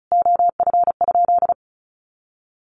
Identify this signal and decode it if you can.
Morse code, heard as a keyed tone, about 34 wpm, 700 Hz. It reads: OF?
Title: Sentences from Top 1400 Words 35wpm